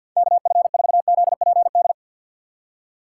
Morse code, heard as a keyed tone, about 52 wpm, 700 Hz. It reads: KF4ZPD
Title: USA Call Signs - 6 Characters - 4x Character Spacing - 50wpm